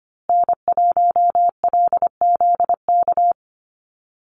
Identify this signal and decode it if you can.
Morse code, heard as a keyed tone, about 25 wpm, 700 Hz. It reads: N1LZK
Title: USA Call Signs Repeated 2x 25wpm